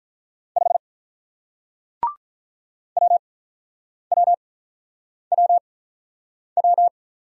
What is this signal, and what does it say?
Morse code, heard as a keyed tone, about 48 wpm, spaced as for 9 wpm, 700 Hz. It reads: 5 WWWW